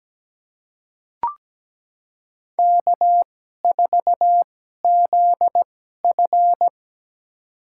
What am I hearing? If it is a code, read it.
Morse code, heard as a keyed tone, about 17 wpm, spaced as for 12 wpm, 700 Hz. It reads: K4ZF
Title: USA Call Signs - 4 Characters - 2x Character Spacing - 17wpm